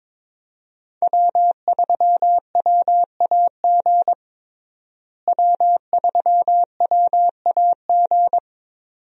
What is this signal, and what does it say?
Morse code, heard as a keyed tone, about 22 wpm, 700 Hz. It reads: W3WAG W3WAG
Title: USA Call Signs - 5 Characters - Repeated 2x - 22wpm